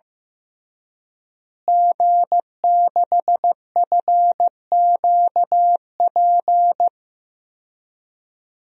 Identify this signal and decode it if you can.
Morse code, heard as a keyed tone, about 15 wpm, 700 Hz. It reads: G6FQP
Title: DXCC-100 Call Signs - 5 Characters - Repeated 3x - 15wpm